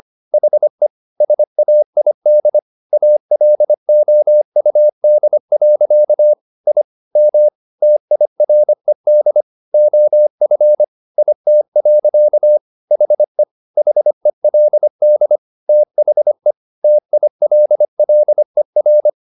Morse code, heard as a keyed tone, about 25 wpm, 600 Hz. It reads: HE SAID ALOUD. I M TIRED OF IT. HE HELD THE TILLER